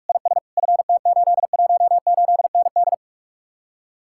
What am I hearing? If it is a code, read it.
Morse code, heard as a keyed tone, about 49 wpm, 700 Hz. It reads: IS FT818ND